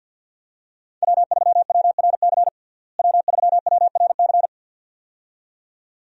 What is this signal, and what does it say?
Morse code, heard as a keyed tone, about 50 wpm, 700 Hz. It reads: W3WRC W3WRC
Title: USA Call Signs - 5 Characters - Repeated 2x - 50wpm